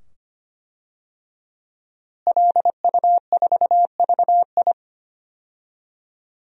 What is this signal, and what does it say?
Morse code, heard as a keyed tone, about 25 wpm, 700 Hz. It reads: LU4VI
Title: DXCC-100 Call Signs - 5 Characters - Repeated 2x - 25wpm